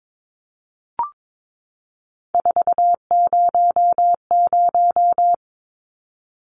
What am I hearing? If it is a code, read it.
Morse code, heard as a keyed tone, about 22 wpm, 700 Hz. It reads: 400
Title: Numbers - 3-Digit 22wpm